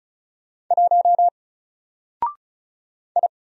Morse code, heard as a keyed tone, about 33 wpm, 700 Hz. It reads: 1 I
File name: Single-Letter-Number-35wpm